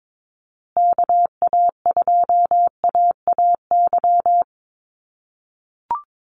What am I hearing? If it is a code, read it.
Morse code, heard as a keyed tone, about 22 wpm, 700 Hz. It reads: KA2AAY
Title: USA Call Signs Repeated 3x 22wpm